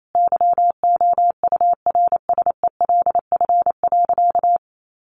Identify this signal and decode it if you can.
Morse code, heard as a keyed tone, about 28 wpm, 700 Hz. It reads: YOURSELF.
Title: Sentences from Top 600 Words 28wpm